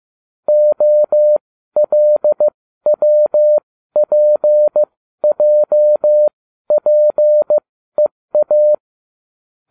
Morse code, heard as a keyed tone, about 15 wpm, spaced as for 12 wpm, 600 Hz. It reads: OLWPJPEA